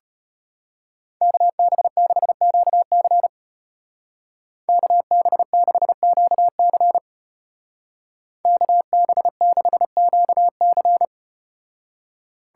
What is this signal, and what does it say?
Morse code, heard as a keyed tone, about 34 wpm, 700 Hz. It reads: KB6QC KB6QC KB6QC